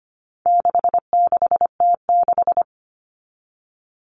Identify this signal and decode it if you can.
Morse code, heard as a keyed tone, about 25 wpm, 700 Hz. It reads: 66T6
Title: Common Cut Numbers - 4-Digit 25wpm